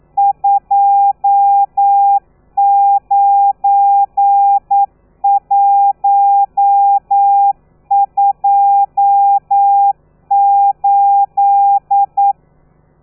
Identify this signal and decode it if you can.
Morse code, heard as a keyed tone, about 9 wpm, 800 Hz. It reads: 29128
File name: telegrafia / i2tfj / Lez12 40